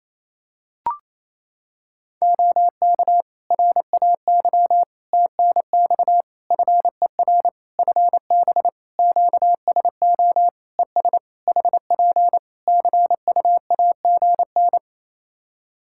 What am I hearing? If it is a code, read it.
Morse code, heard as a keyed tone, about 28 wpm, 700 Hz. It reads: OK RAY TNX FER FB QSO ES HP CUAGN
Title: Practice QSO - Exchange 4 - Ending - 28wpm